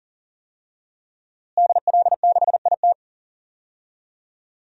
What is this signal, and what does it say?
Morse code, heard as a keyed tone, about 40 wpm, 700 Hz. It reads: DL6IT
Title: DXCC-100 Call Signs - 5 Characters - Repeated 2x - 40wpm